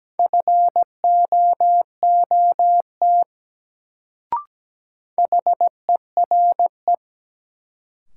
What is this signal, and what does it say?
Morse code, heard as a keyed tone, about 17 wpm, 700 Hz. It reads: FOOT HERE
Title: Sets of 5 Words - New Word Drill - 17wpm